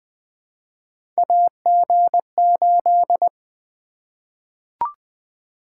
Morse code, heard as a keyed tone, about 20 wpm, 700 Hz. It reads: AG8
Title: Three Letter-Number Combinations - 20wpm